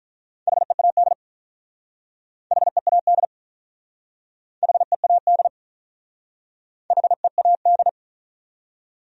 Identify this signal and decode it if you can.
Morse code, heard as a keyed tone, about 46 wpm, 700 Hz. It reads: HEAD HEAD HEAD HEAD